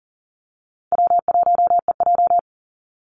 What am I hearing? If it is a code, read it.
Morse code, heard as a keyed tone, about 40 wpm, 700 Hz. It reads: W1EJ
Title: NA SOTA Activator Callsigns 40wpm